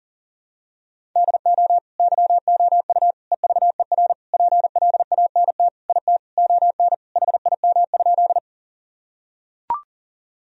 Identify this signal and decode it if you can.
Morse code, heard as a keyed tone, about 40 wpm, 700 Hz. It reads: DO YOU EVER PLANT IT ON HIM?
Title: Sentences from Top 1000 Words - Review Part 1 - 40wpm